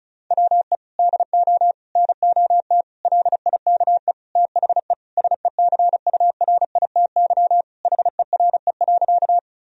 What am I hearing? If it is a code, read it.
Morse code, heard as a keyed tone, about 36 wpm, 700 Hz. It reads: WE DO NOT LIKE THE SECURITY HERE.
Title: Sentences from Top 1000 Words - Review Part 4 - 35wpm